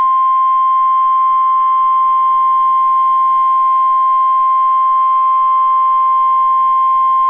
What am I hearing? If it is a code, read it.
Emergency Alert System (SAME) tones (the 1050 Hz attention tone). It is heard from 0.4 to 6.3 s.